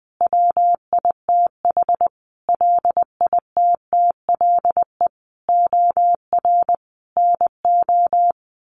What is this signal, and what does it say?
Morse code, heard as a keyed tone, about 20 wpm, 700 Hz. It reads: WITH LITTLE OR NO
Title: Sentences from Top 1400 Words 20wpm